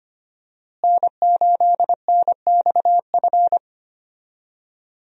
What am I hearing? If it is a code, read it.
Morse code, heard as a keyed tone, about 25 wpm, 700 Hz. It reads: N8NXF
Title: USA Call Signs - 5 Characters - Repeated 2x - 25wpm